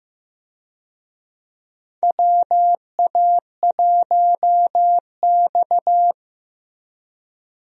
Morse code, heard as a keyed tone, about 15 wpm, 700 Hz. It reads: WA1X
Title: USA Call Signs - 4 Characters - Repeated 2x - 15wpm